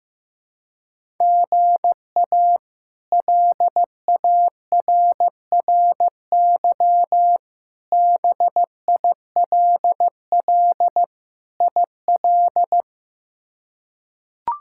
Morse code, heard as a keyed tone, about 16 wpm, 700 Hz. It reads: GA LARRY BILL IL
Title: K1USN SST QSO Practice - 15wpm